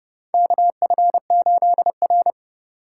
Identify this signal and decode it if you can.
Morse code, heard as a keyed tone, about 30 wpm, 700 Hz. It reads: KF8R
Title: USA Call Signs - 4 Characters - 2x Character Spacing - 30wpm